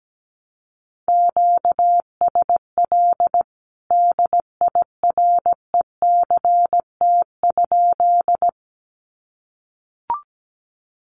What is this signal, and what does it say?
Morse code, heard as a keyed tone, about 17 wpm, 700 Hz. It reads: QSL DIRECT?
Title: Common QSO Phrases 17wpm